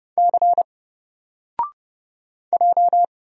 Morse code, heard as a keyed tone, about 30 wpm, 700 Hz. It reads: C J
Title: Single Letter-Number 30wpm